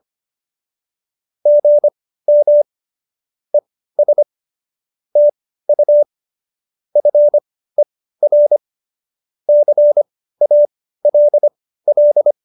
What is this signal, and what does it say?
Morse code, heard as a keyed tone, about 25 wpm, spaced as for 15 wpm, 590 Hz. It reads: GM ES TU FER CALL
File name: qso222 15